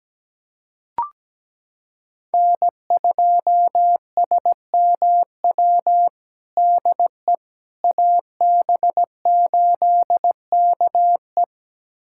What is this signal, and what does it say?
Morse code, heard as a keyed tone, about 17 wpm, 700 Hz. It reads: N2SMW DE AB8KE
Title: USA Callsign DE Callsign - 5 Characters - 17wpm